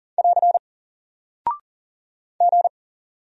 Morse code, heard as a keyed tone, about 39 wpm, 700 Hz. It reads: <AR> G
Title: Single Character (Letter:Number:Punctuation:Prosign) Alt 40wpm